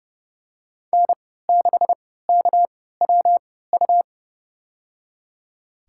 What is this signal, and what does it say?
Morse code, heard as a keyed tone, about 30 wpm, spaced as for 17 wpm, 700 Hz. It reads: N6KWU
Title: USA Call Signs - 5 Characters - 3x Character Spacing - 30wpm